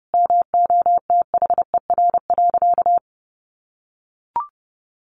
Morse code, heard as a keyed tone, about 30 wpm, 700 Hz. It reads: MOTHER.
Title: Sentences from Top 700 Words 30wpm